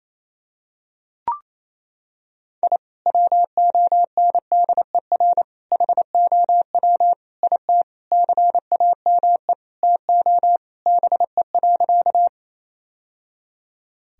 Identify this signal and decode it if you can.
Morse code, heard as a keyed tone, about 28 wpm, 700 Hz. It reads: I WONDER HOW IT CAME TO BE.
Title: Sentences from Top 1000 Words - Review Part 3 - 28wpm